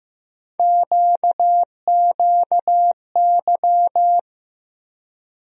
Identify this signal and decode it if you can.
Morse code, heard as a keyed tone, about 15 wpm, 700 Hz. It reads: QQY
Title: USA Call Sign Postfixes - 3 Letters 15wpm